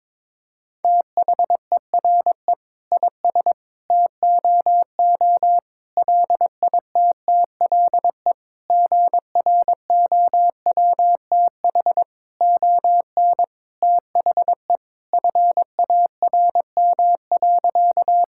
Morse code, heard as a keyed tone, about 22 wpm, 700 Hz. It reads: THERE IS TOO LITTLE GROWTH ON THE FARM.
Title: Sentences from Top 1000 Words - Review Part 5 - 22wpm